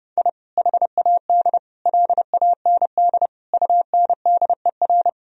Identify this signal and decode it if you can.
Morse code, heard as a keyed tone, about 30 wpm, 700 Hz. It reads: I HAD LAND UNDER
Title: Sentences from Top 1000 Words - Review Part 1 - 30wpm